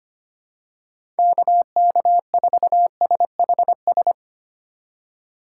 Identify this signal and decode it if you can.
Morse code, heard as a keyed tone, about 25 wpm, 700 Hz. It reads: KK4SHS
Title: USA Call Signs - 6 Characters - 3x Character Spacing - 25wpm